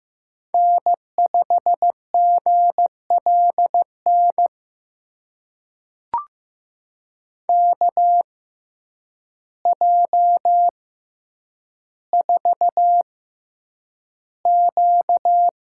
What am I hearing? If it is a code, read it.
Morse code, heard as a keyed tone, about 15 wpm, spaced as for 8 wpm, 700 Hz. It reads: N5GLN KJ4Q
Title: USA Call Signs - 5 Characters - 6x Character Spacing - 15wpm